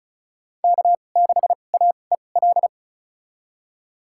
Morse code, heard as a keyed tone, about 34 wpm, spaced as for 25 wpm, 690 Hz. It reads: K6AEL
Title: USA Call Signs - 5 Characters - 2x Character Spacing - 35wpm